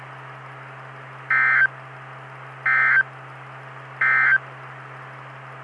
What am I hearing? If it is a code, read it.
Emergency Alert System (SAME) tones, heard as harsh buzzing bursts. It is heard from 1.2 to 4.5 s.
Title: Saline, AK SKYWARN Repeater System - W5RHS